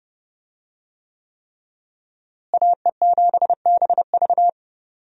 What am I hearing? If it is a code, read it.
Morse code, heard as a keyed tone, about 30 wpm, 700 Hz. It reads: AE7BV
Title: USA Call Signs - 5 Characters - Repeated 3x - 30wpm